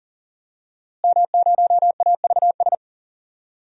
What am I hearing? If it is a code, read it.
Morse code, heard as a keyed tone, about 40 wpm, 700 Hz. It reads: M0AVS
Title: DXCC-100 Call Signs - 5 Characters - Repeated 3x - 40wpm